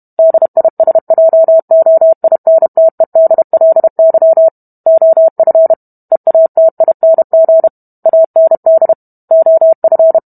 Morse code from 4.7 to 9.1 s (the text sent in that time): OF EATING AND